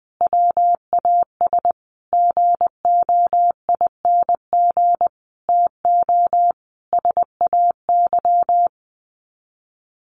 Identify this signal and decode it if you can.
Morse code, heard as a keyed tone, about 20 wpm, 700 Hz. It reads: WAS GOING TO SAY
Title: Sets of 5 Words 20wpm